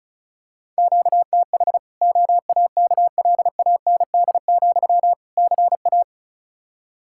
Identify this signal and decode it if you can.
Morse code, heard as a keyed tone, about 36 wpm, 700 Hz. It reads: QTH OAKLAND, CA
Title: K1USN SST QSO Practice 35wpm